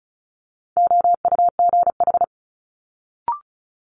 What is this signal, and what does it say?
Morse code, heard as a keyed tone, about 36 wpm, 700 Hz. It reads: OUGH
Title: Multi-letter Phonemes Quick Fire 35wpm